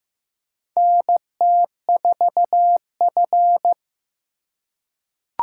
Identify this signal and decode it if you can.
Morse code, heard as a keyed tone, about 15 wpm, 700 Hz. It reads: NT4F
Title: USA Call Signs - 4 Characters - 2x Character Spacing - 15wpm